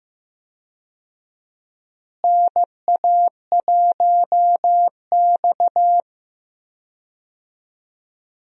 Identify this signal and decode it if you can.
Morse code, heard as a keyed tone, about 15 wpm, 700 Hz. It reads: NA1X
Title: USA Call Signs - 4 Characters - Repeated 2x - 15wpm